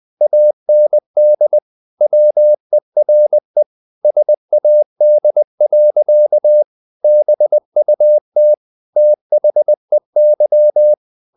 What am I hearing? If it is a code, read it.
Morse code, heard as a keyed tone, about 20 wpm, 600 Hz. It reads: AND WERE SAD. BUT THEY
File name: the old man and the sea / the old man and the sea part 1 20wpm